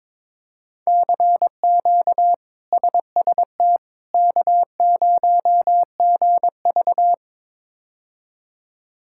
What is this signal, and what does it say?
Morse code, heard as a keyed tone, about 22 wpm, 700 Hz. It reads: CQ SST K0GV
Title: K1USN SST QSO Practice - 22wpm